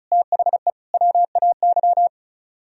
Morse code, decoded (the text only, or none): THE WAY